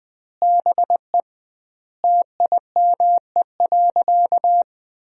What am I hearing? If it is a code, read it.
Morse code, heard as a keyed tone, about 20 wpm, 700 Hz. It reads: BE TIME.
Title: 2x Word-Spacing - Sentences from Top 200 Words - 20wpm